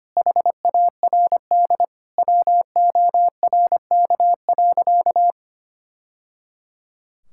Morse code, heard as a keyed tone, about 25 wpm, 700 Hz. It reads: HARD WORK.
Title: Sentences from Top 1000 Words - Review Part 5 - 25wpm